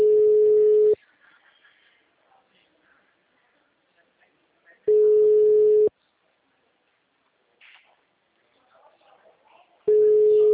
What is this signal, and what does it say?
Ringback tone (Europe, 425 Hz), with speech.
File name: out- -2019- - - .47123